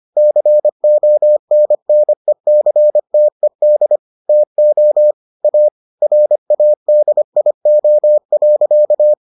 Morse code, heard as a keyed tone, about 25 wpm, 600 Hz. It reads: CONNECTED TO A RADIO.